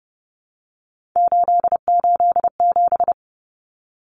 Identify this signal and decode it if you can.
Morse code, heard as a keyed tone, about 30 wpm, 700 Hz. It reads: 887
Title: Common Cut Numbers - 3-Digit 30wpm